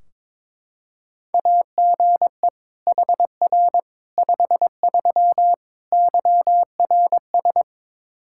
Morse code, heard as a keyed tone, about 22 wpm, 700 Hz. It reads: AGE HR 53 YRS
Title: Practice QSO - Exchange 3 - More Info - 22wpm